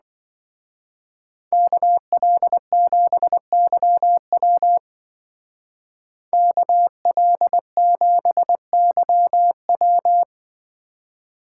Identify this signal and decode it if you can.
Morse code, heard as a keyed tone, about 24 wpm, 700 Hz. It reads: KL7YW KL7YW
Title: Speed Racing - USA Call Signs 20wpm